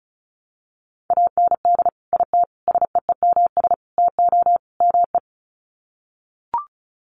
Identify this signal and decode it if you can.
Morse code, heard as a keyed tone, about 34 wpm, 700 Hz. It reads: AND IT SEEMS TO ME
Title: Sets of 5 Words Encore2 35wpm